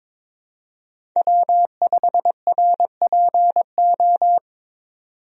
Morse code, heard as a keyed tone, about 22 wpm, 700 Hz. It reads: W5RPO
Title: USA Call Signs - 5 Characters - 22wpm